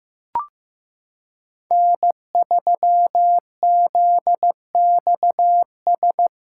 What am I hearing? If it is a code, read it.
Morse code, heard as a keyed tone, about 15 wpm, 700 Hz. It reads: N3ZXS
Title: USA Callsign DE Callsign - 5 Characters - 15wpm